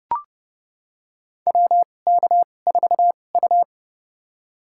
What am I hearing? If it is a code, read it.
Morse code, heard as a keyed tone, about 30 wpm, spaced as for 22 wpm, 700 Hz. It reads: WK4U
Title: USA Call Signs - 4 Characters - 2x Character Spacing - 30wpm